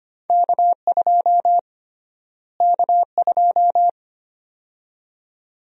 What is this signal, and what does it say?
Morse code, heard as a keyed tone, about 25 wpm, 700 Hz. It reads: K2 K2
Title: USA Call Sign Prefixes - Repeated 2x - 25wpm